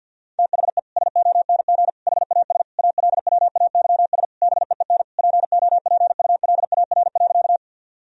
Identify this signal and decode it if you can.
Morse code, heard as a keyed tone, about 50 wpm, 700 Hz. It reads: THE SONG HAS ALWAYS BEEN POPULAR.
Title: Sentences from Top 1000 Words - Review Part 3 - 50wpm